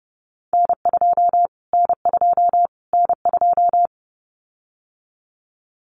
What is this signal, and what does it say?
Morse code, heard as a keyed tone, about 30 wpm, 700 Hz. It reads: N2 N2 N2
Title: Callsign Prefixes for Top 25 DXCC - Repeated 3x - 30wpm